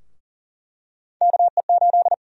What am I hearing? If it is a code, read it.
Morse code, heard as a keyed tone, about 40 wpm, 700 Hz. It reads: KE8
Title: USA Call Sign Prefixes - Repeated 2x - 40wpm